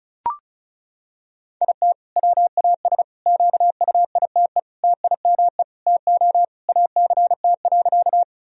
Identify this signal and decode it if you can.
Morse code, heard as a keyed tone, about 34 wpm, 700 Hz. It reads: IT WAS QUITE TIME TO ACT.
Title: Sentences from Top 1000 Words - Review Part 1 - 35wpm